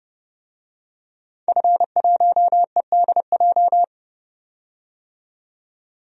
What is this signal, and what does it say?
Morse code, heard as a keyed tone, about 30 wpm, 700 Hz. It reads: F1EDJ
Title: DXCC-100 Call Signs - 5 Characters - 30wpm